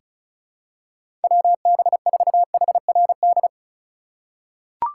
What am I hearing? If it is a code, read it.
Morse code, heard as a keyed tone, about 34 wpm, 700 Hz. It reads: WB4HRD
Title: USA Call Signs - 6 Characters - 2x Character Spacing - 35wpm